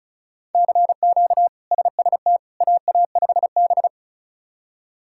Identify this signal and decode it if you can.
Morse code, heard as a keyed tone, about 34 wpm, 700 Hz. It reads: CQ SST AA5B
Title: K1USN SST QSO Practice - 35wpm